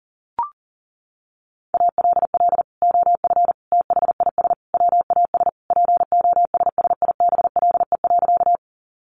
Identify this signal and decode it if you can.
Morse code, heard as a keyed tone, about 37 wpm, 700 Hz. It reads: ALL OF THIS WAS POSSIBLE.
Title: Sentences from Top 500 Words 40wpm